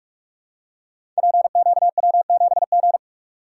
Morse code, heard as a keyed tone, about 44 wpm, 700 Hz. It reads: PQW7G
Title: Code Groups - 5 Letter:Numbers - 45wpm